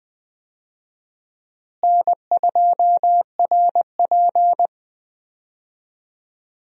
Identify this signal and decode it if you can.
Morse code, heard as a keyed tone, about 20 wpm, 700 Hz. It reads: N2RP
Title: USA Call Signs - 4 Characters - Repeated 3x - 20wpm